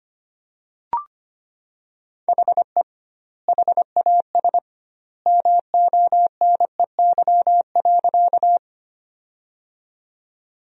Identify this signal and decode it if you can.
Morse code, heard as a keyed tone, about 25 wpm, 700 Hz. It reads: HE HAS MONEY.
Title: 2x Word-Spacing - Sentences from Top 400 Words - 25wpm